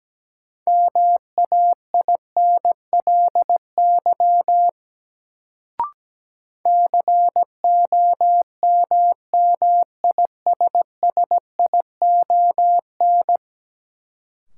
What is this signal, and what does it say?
Morse code, heard as a keyed tone, about 17 wpm, 700 Hz. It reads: MAINLY COMMISSION
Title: Sentences from Top 1400 Words - New Word Drill Alt - 17wpm